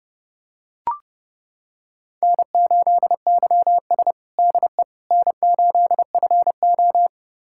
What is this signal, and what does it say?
Morse code, heard as a keyed tone, about 30 wpm, 700 Hz. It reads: N8YS DE N8FO
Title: USA Callsign DE Callsign - 4 Characters - 30wpm